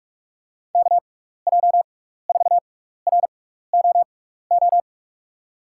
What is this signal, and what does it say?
Morse code, heard as a keyed tone, about 46 wpm, spaced as for 15 wpm, 700 Hz. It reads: KJ4ROO
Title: USA Call Signs - 6 Characters - 6x Character Spacing - 45wpm